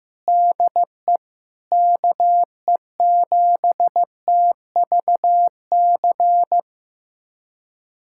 Morse code, heard as a keyed tone, about 15 wpm, 700 Hz. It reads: DE KE7TVC
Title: USA Callsign DE Callsign - 6 Characters - 15wpm